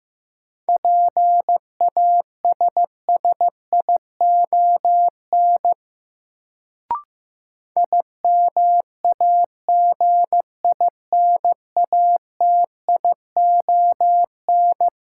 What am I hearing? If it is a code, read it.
Morse code, heard as a keyed tone, about 15 wpm, 700 Hz. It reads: PASSION IMAGINATION
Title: Sentences from Top 1300 Words - New Word Drill - 15wpm